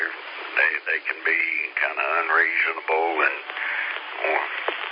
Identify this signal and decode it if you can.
Emergency Alert System (SAME) tones, heard as harsh buzzing bursts, with speech. It is heard from 1.1 to 3.9 s.